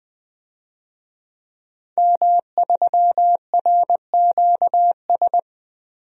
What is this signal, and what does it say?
Morse code, heard as a keyed tone, about 20 wpm, 700 Hz. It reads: M3RQS
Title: DXCC-100 Call Signs - 5 Characters - 20wpm